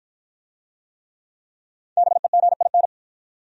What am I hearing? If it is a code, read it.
Morse code, heard as a keyed tone, about 53 wpm, 700 Hz. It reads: BEGIN